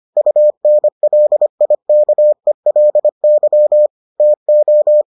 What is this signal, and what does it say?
Morse code, heard as a keyed tone, about 25 wpm, 600 Hz. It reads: UNLIKELY TO